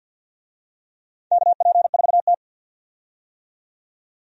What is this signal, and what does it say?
Morse code, heard as a keyed tone, about 53 wpm, 700 Hz. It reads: KP4T
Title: DXCC-100 Call Signs - 4 Characters - 50wpm